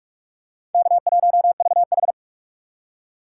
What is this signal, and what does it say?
Morse code, heard as a keyed tone, about 40 wpm, 700 Hz. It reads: K1VH